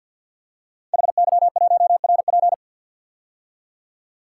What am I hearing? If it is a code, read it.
Morse code, heard as a keyed tone, about 50 wpm, 700 Hz. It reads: HY1RP
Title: Code Groups - 5 Letter:Numbers - 50wpm